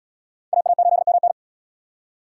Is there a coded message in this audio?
Yes: Morse code from 0.5 to 1.3 s, heard as a keyed tone, about 75 wpm, 700 Hz. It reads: AA8MN